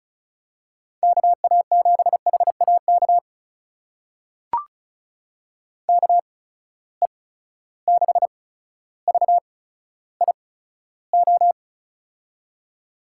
Morse code, heard as a keyed tone, about 34 wpm, spaced as for 15 wpm, 700 Hz. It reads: KA7HAK KE6VIO
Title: USA Call Signs - 6 Characters - 8x Character Spacing - 35wpm